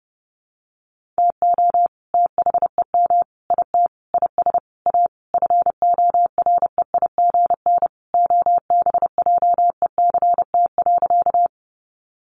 Morse code, heard as a keyed tone, about 30 wpm, 700 Hz. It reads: TO THEM IT IS A FOREIGN OBJECT.
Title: Sentences from Top 1400 Words 30wpm